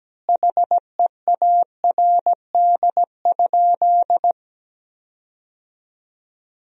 Morse code, heard as a keyed tone, about 17 wpm, 700 Hz. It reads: HEARD?
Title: 2x Word-Spacing - Sentences from Top 400 Words - 17wpm